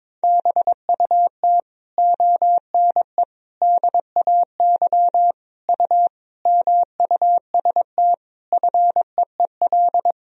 Morse code, heard as a keyed tone, about 21 wpm, 700 Hz. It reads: BUT ONE DAY U MUST FEEL
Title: Sentences from Top 400 Words and Abbreviations 22wpm